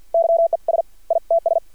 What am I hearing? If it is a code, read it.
Morse code, heard as a keyed tone, about 50 wpm, 670 Hz. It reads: YES ITS
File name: archived / CRSnet